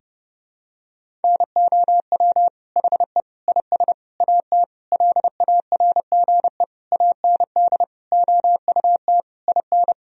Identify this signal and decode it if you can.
Morse code, heard as a keyed tone, about 30 wpm, 700 Hz. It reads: NOW HE IS AT LARGE AND OUT IN
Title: Sentences from Top 1000 Words - Review Part 1 - 30wpm